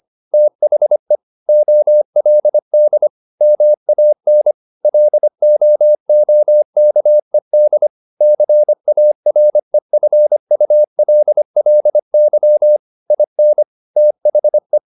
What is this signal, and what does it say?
Morse code, heard as a keyed tone, about 25 wpm, 600 Hz. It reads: THE OLD MAN LOOKED CAREFULLY IN THE